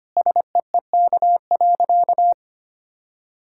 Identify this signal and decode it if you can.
Morse code, heard as a keyed tone, about 25 wpm, 690 Hz. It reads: SEEK.
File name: Easy Sentences with Hints - 25wpm